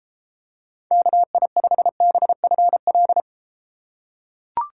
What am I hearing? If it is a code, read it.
Morse code, heard as a keyed tone, about 33 wpm, 700 Hz. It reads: KI5BFL